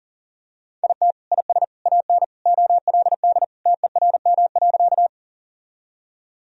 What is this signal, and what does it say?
Morse code, heard as a keyed tone, about 40 wpm, 690 Hz. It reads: IT IS AN OLD TERM.
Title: Sentences from Top 1000 Words - Review Part 4 - 40wpm